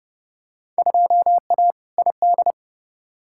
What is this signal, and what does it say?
Morse code, heard as a keyed tone, about 29 wpm, 700 Hz. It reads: 2A ID
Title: Field Day Exchanges - 30wpm